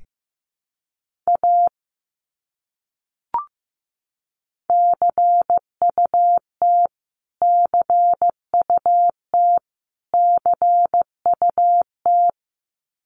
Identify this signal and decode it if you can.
Morse code, heard as a keyed tone, about 15 wpm, 700 Hz. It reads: A CUT CUT CUT
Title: Top 300 Words Repeated 15wpm